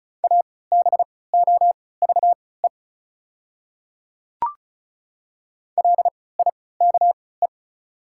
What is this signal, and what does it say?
Morse code, heard as a keyed tone, about 36 wpm, spaced as for 20 wpm, 690 Hz. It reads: ABOVE LIKE
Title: 3x Character Spacing - Top 500 Words - 35wpm